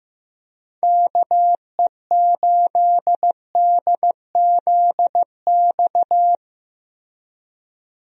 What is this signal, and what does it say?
Morse code, heard as a keyed tone, about 15 wpm, 700 Hz. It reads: KE8DZX
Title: USA Call Signs - 6 Characters - Repeated 3x - 15wpm